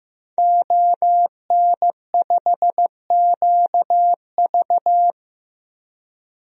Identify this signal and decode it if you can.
Morse code, heard as a keyed tone, about 15 wpm, 700 Hz. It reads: ON5QV
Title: DXCC-100 Call Signs - 5 Characters - Repeated 2x - 15wpm